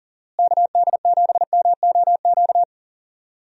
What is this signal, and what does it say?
Morse code, heard as a keyed tone, about 40 wpm, 700 Hz. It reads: KD7MOQ
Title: USA Call Signs - 6 Characters - Repeated 3x - 40wpm